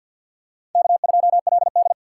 Morse code, heard as a keyed tone, about 50 wpm, 700 Hz. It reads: K2LD
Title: USA Call Signs - 4 Characters - 6x Character Spacing - 50wpm